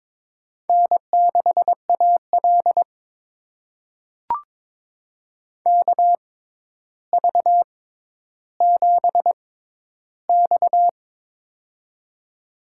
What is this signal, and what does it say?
Morse code, heard as a keyed tone, about 22 wpm, spaced as for 11 wpm, 700 Hz. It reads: N6AL KV7X
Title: USA Call Signs - 4 Characters - 6x Character Spacing - 22wpm